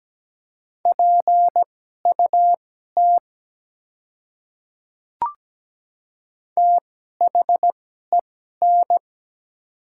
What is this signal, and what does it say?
Morse code, heard as a keyed tone, about 17 wpm, spaced as for 12 wpm, 700 Hz. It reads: PUT THEN
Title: 2x Character Spacing - Top 200 Words - 17wpm